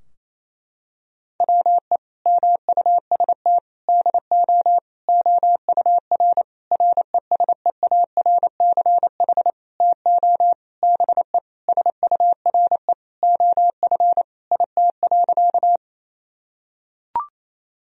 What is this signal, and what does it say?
Morse code, heard as a keyed tone, about 28 wpm, 700 Hz. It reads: WE MUST DO OUR RESEARCH TO BE SURE OF IT.
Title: Sentences from Top 1000 Words - Review Part 4 - 28wpm